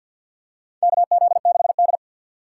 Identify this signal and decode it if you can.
Morse code, heard as a keyed tone, about 50 wpm, 700 Hz. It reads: KZ6D
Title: USA Call Signs - 4 Characters - 2x Character Spacing - 50wpm